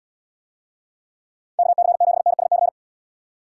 Morse code, heard as a keyed tone, about 72 wpm, 700 Hz. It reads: KC7NSC